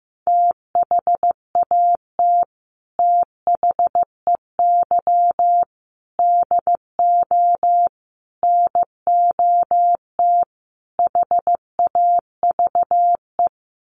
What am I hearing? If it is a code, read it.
Morse code, heard as a keyed tone, about 15 wpm, 700 Hz. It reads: THAT THEY DO NOT HAVE
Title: Sets of 5 Words Encore2 15wpm